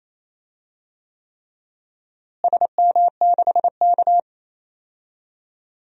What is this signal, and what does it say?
Morse code, heard as a keyed tone, about 28 wpm, 700 Hz. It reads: SM6K
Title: DXCC-100 Call Signs - 4 Characters - Repeated 3x - 28wpm